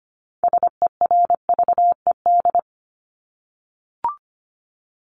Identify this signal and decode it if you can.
Morse code, heard as a keyed tone, about 25 wpm, 700 Hz. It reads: SERVED
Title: ED Suffix - Top 100 Words with Base Words 25wpm